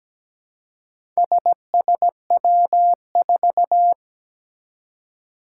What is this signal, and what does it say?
Morse code, heard as a keyed tone, about 17 wpm, 700 Hz. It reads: SSW4
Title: Four Challenging Letter-Number Combinations 17wpm